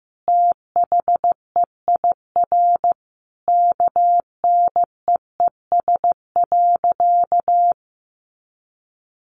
Morse code, heard as a keyed tone, about 15 wpm, 700 Hz. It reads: THEIR KNEES.
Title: A Sound Of Thunder 15wpm